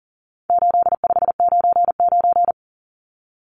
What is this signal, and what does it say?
Morse code, heard as a keyed tone, about 40 wpm, 700 Hz. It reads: 8599
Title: Numbers - 4-Digit 40wpm